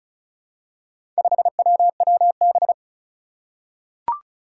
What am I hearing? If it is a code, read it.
Morse code, heard as a keyed tone, about 34 wpm, 700 Hz. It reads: 5WWB
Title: Four Challenging Letter-Number Combinations 35wpm